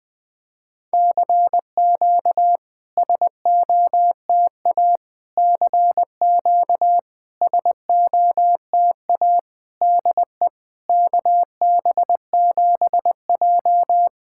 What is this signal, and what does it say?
Morse code, heard as a keyed tone, about 20 wpm, 700 Hz. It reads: CQ SOTA CQ SOTA DE KB7J
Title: SOTA QSO Practice 20wpm